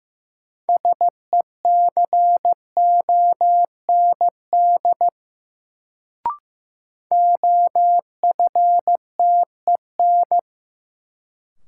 Morse code, heard as a keyed tone, about 15 wpm, 700 Hz. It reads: SECOND OFTEN
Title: Sentences from Top 300 Words - New Word Drill - 15wpm